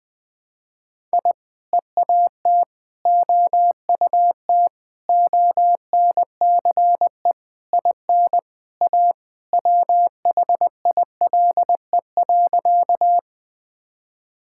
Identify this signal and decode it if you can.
Morse code, heard as a keyed tone, about 20 wpm, 700 Hz. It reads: I EAT OUT ONCE IN A WHILE.
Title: Sentences with 4-Word Phrases - 20wpm